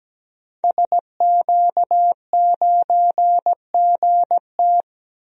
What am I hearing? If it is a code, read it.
Morse code, heard as a keyed tone, about 17 wpm, 700 Hz. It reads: SQ9GT
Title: DXCC-100 Call Signs - 5 Characters - Repeated 3x - 17wpm